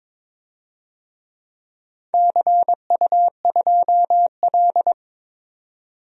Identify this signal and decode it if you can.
Morse code, heard as a keyed tone, about 23 wpm, 700 Hz. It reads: CU2L
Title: DXCC-100 Call Signs - 4 Characters - 22wpm